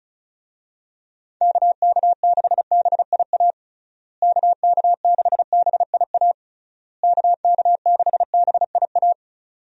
Morse code, heard as a keyed tone, about 36 wpm, 700 Hz. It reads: KK6BIA KK6BIA KK6BIA
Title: USA Call Signs - 6 Characters - Repeated 3x - 35wpm